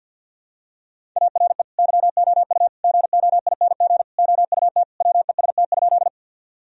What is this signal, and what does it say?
Morse code, heard as a keyed tone, about 53 wpm, 700 Hz. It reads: ARE YOU GOING OUT WEST?
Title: Sentences from Top 1000 Words - Review Part 2 - 50wpm